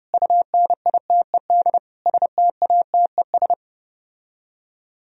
Morse code, heard as a keyed tone, about 33 wpm, 700 Hz. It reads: UNITED STATES